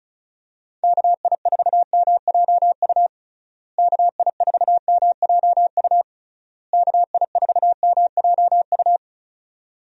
Morse code, heard as a keyed tone, about 34 wpm, 700 Hz. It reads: KI4MJU KI4MJU KI4MJU
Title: USA Call Signs - 6 Characters - Repeated 3x - 35wpm